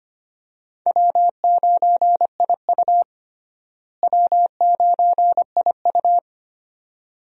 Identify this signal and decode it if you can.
Morse code, heard as a keyed tone, about 25 wpm, 700 Hz. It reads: W9IU W9IU
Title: USA Call Signs - 4 Characters - Repeated 2x - 25wpm